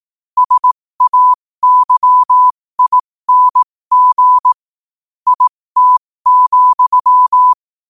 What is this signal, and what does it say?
Morse code, heard as a keyed tone, about 18 wpm, spaced as for 15 wpm, 1000 Hz. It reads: SAYING IT,
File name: qotdfiles / 15wpm